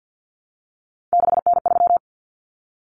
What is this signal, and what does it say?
Morse code, heard as a keyed tone, about 50 wpm, 700 Hz. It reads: 6N3
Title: Common Cut Numbers - 3-Digit 50wpm